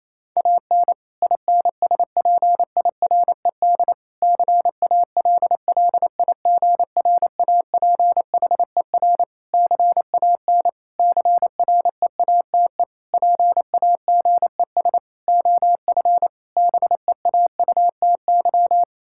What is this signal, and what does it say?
Morse code, heard as a keyed tone, about 28 wpm, 700 Hz. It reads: AN INSPIRED CALLIGRAPHER CAN CREATE PAGES OF BEAUTY
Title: Pangram Challenge v2 - 28wpm